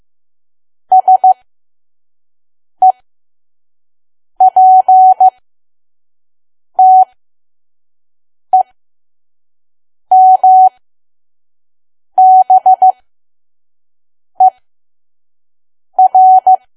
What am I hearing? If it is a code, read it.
Morse code, heard as a keyed tone, about 15 wpm, spaced as for 5 wpm, 750 Hz. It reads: SEPTEMBER
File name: Archive / 5 WPM files / PM